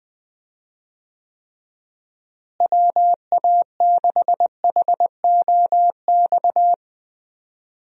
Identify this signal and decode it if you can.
Morse code, heard as a keyed tone, about 20 wpm, 700 Hz. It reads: WA6HOX
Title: USA Call Signs - 6 Characters - Repeated 2x - 20wpm